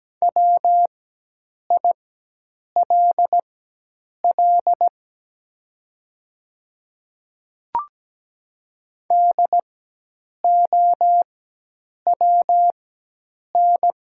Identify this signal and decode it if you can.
Morse code, heard as a keyed tone, about 17 wpm, spaced as for 8 wpm, 700 Hz. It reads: WILL DOWN
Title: 4x Character Spacing - Top 300 Words - 17wpm